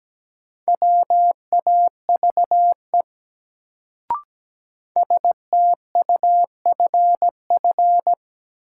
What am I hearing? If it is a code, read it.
Morse code, heard as a keyed tone, about 17 wpm, 700 Hz. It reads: WAVE STUFF
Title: Sentences from Top 1400 Words - New Word Drill Alt - 17wpm